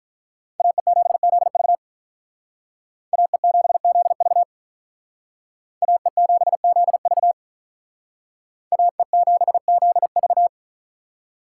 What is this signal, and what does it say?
Morse code, heard as a keyed tone, about 43 wpm, 700 Hz. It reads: AE7ZV AE7ZV AE7ZV AE7ZV
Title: Speed Racing - USA 2x2 Call Signs 35wpm